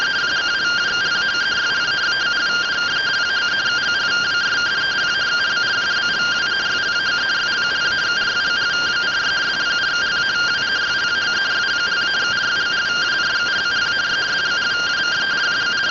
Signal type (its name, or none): two-tone teleprinter signal